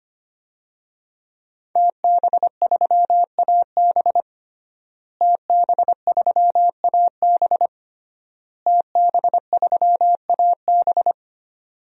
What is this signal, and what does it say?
Morse code, heard as a keyed tone, about 25 wpm, 700 Hz. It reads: TB3AB TB3AB TB3AB
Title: DXCC-100 Call Signs - 5 Characters - Repeated 3x - 25wpm